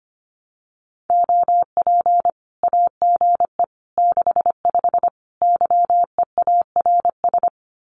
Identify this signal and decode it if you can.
Morse code, heard as a keyed tone, about 25 wpm, 700 Hz. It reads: OP AGE 65 YEARS
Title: Common QSO Phrases 25wpm